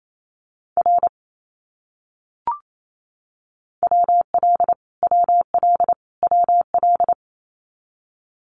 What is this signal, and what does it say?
Morse code, heard as a keyed tone, about 28 wpm, 700 Hz. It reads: R WL WL WL
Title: Abbreviations Repeated 3x - 28wpm